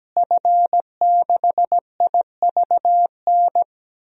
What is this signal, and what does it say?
Morse code, heard as a keyed tone, about 17 wpm, 700 Hz. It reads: F6IVN
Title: DXCC-100 Call Signs - 5 Characters - Repeated 3x - 17wpm